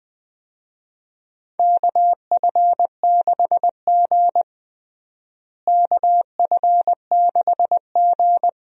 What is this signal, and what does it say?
Morse code, heard as a keyed tone, about 21 wpm, 700 Hz. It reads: KF6G KF6G
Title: USA Call Signs - 4 Characters - Repeated 3x - 20wpm